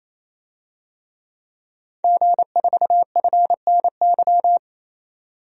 Morse code, heard as a keyed tone, about 28 wpm, 700 Hz. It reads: G4FNY
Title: DXCC-100 Call Signs - 5 Characters - 28wpm